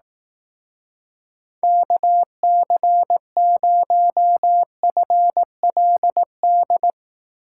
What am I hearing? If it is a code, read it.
Morse code, heard as a keyed tone, about 18 wpm, 700 Hz. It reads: KC0FLD
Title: Speed Racing - USA 2x3 Call Signs 15wpm